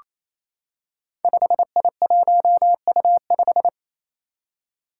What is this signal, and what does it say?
Morse code, heard as a keyed tone, about 28 wpm, 700 Hz. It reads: 5I1U5
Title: Five Challenging Letter-Number Combinations 28wpm